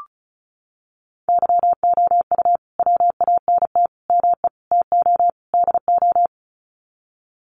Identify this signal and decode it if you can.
Morse code, heard as a keyed tone, about 34 wpm, 700 Hz. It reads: YOU WANT ME TO DO
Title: Sets of 5 Words Encore2 35wpm